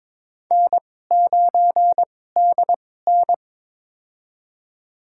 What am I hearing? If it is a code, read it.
Morse code, heard as a keyed tone, about 22 wpm, spaced as for 16 wpm, 700 Hz. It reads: N9DN
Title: USA Call Signs - 2x Character Spacing - 22wpm